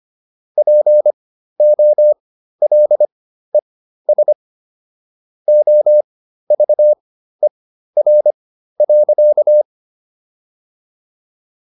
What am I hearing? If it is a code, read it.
Morse code, heard as a keyed tone, about 25 wpm, spaced as for 13 wpm, 600 Hz. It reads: POLES OVER.